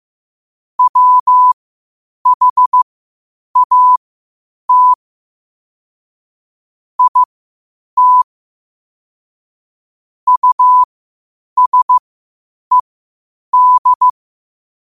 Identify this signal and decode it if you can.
Morse code, heard as a keyed tone, about 15 wpm, spaced as for 8 wpm, 1000 Hz. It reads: WHAT IT USED
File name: qotdfiles / 8wpm